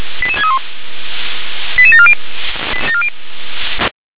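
Five-tone selective call (ZVEI), heard as quick runs of five tones, tones of about 70 ms. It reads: 88631 88740
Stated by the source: ZVEI five-tone selective call received on an SDR.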